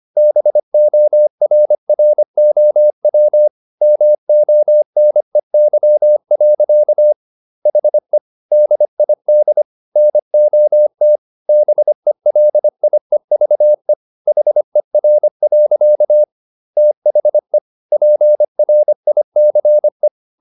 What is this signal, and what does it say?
Morse code, heard as a keyed tone, about 25 wpm, 600 Hz. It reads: BORROW MONEY. HE DID NOT BELIEVE HER. THE PRICE